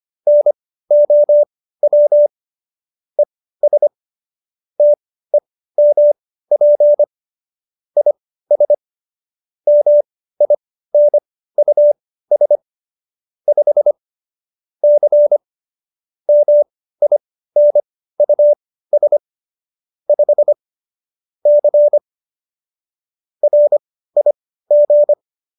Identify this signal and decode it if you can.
Morse code, heard as a keyed tone, about 25 wpm, spaced as for 15 wpm, 590 Hz. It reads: NOW ES TEMP IS MINUS 5 C MINUS 5 C RIG